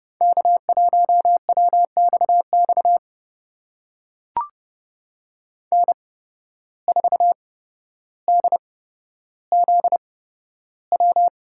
Morse code, heard as a keyed tone, about 30 wpm, spaced as for 13 wpm, 700 Hz. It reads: K1WXX N4DZW
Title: USA Call Signs - 5 Characters - 8x Character Spacing - 30wpm